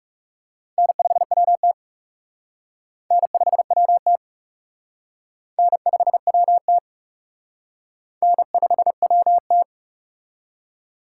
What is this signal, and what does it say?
Morse code, heard as a keyed tone, about 37 wpm, 700 Hz. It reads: N5WT N5WT N5WT N5WT
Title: Speed Racing - USA 1x2 Call Signs 30wpm